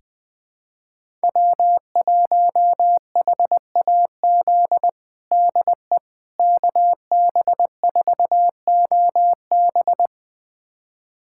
Morse code, heard as a keyed tone, about 21 wpm, 700 Hz. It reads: W1HAZ DE KB4OB
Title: USA Callsign DE Callsign - 5 Characters - 20wpm